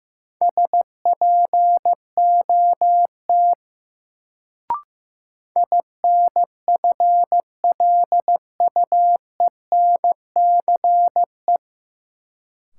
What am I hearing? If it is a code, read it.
Morse code, heard as a keyed tone, about 15 wpm, 700 Hz. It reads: SPOT INFLUENCE
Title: Sentences from Top 1100 Words - New Word Drill Alt - 15wpm